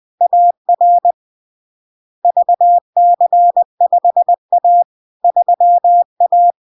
Morse code, heard as a keyed tone, about 20 wpm, 700 Hz. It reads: AR VC5A 3A
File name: LARK-FD-2022-Log-in-CW-20wpm-700-Hz